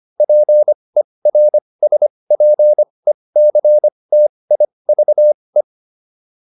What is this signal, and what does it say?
Morse code, heard as a keyed tone, about 25 wpm, spaced as for 20 wpm, 600 Hz. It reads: PERSPECTIVE
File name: SS306 20